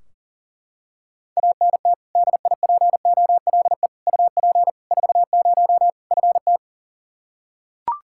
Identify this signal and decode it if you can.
Morse code, heard as a keyed tone, about 40 wpm, 700 Hz. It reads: ANT DIPOLE UP 40 FT
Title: Practice QSO - Exchange 2 - Info - 40wpm